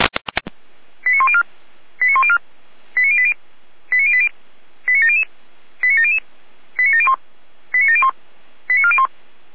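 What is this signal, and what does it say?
Five-tone selective call (ZVEI), heard as quick runs of five tones, tones of about 70 ms. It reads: 89104 89104 89080 89080 89700 89700 89701 89701 89401
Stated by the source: SDR-received ZVEI five-tone call 89115.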